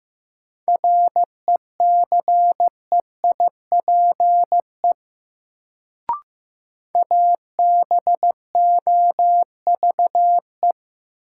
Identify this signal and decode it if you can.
Morse code, heard as a keyed tone, about 15 wpm, 700 Hz. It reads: RECEIPE ABOVE
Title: Sentences from Top 1200 Words - New Word Drill Alt - 15wpm